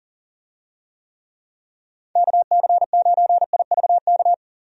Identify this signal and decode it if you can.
Morse code, heard as a keyed tone, about 40 wpm, 700 Hz. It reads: KC9IVK
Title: USA Call Signs - 6 Characters - Repeated 3x - 40wpm